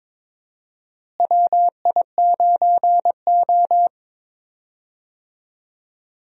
Morse code, heard as a keyed tone, about 22 wpm, 700 Hz. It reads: WI9O
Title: USA Call Signs - 4 Characters - Repeated 2x - 22wpm